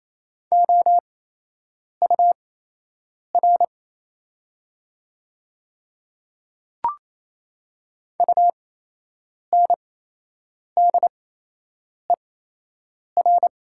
Morse code, heard as a keyed tone, about 28 wpm, spaced as for 8 wpm, 700 Hz. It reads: OUR UNDER
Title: 8x Character Spacing - Top 200 Words - 28wpm